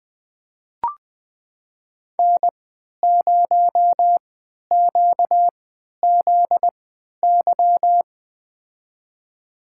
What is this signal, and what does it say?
Morse code, heard as a keyed tone, about 21 wpm, spaced as for 12 wpm, 700 Hz. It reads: N0QZY
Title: USA Call Signs - 3x Character Spacing - 20wpm